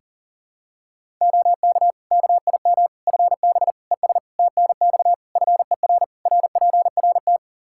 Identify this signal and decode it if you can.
Morse code, heard as a keyed tone, about 40 wpm, 700 Hz. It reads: OK KIM FB ES TNX FER RPRT
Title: Practice QSO - Exchange 2 - Info - 40wpm